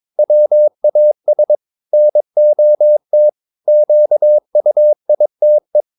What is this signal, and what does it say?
Morse code, heard as a keyed tone, about 22 wpm, 600 Hz. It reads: WAS NOT QUITE
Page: Romanen Never af Ken Follet på engelsk.